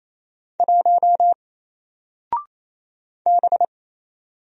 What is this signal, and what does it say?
Morse code, heard as a keyed tone, about 27 wpm, 700 Hz. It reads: 1 B
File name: Single-Letter-Number-28wpm